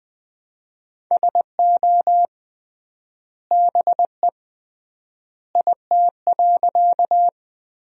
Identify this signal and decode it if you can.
Morse code, heard as a keyed tone, about 20 wpm, 700 Hz. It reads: SO BE IT.
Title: 3x Word-Spacing - Sentences from Top 100 Words - 20wpm